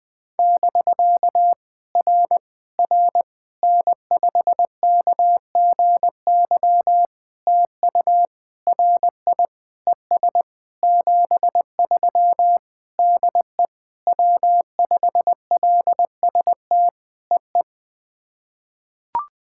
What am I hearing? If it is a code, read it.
Morse code, heard as a keyed tone, about 20 wpm, 690 Hz. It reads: <BK> R R N5KGY TU RI ES 73 DE W5LST EE
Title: POTA QSO Practice - 20wpm